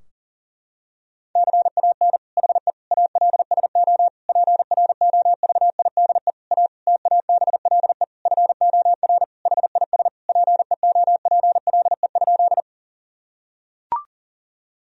Morse code, heard as a keyed tone, about 40 wpm, 700 Hz. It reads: CAN HE ALSO PROVIDE A TABLE FOR HIS PEOPLE?
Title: Sentences from Top 1000 Words - Review Part 3 - 40wpm